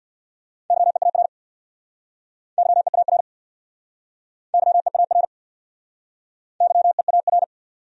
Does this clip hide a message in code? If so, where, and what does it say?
Yes: Morse code from 0.7 to 7.4 s, heard as a keyed tone, about 66 wpm, 700 Hz. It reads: YEAR YEAR YEAR YEAR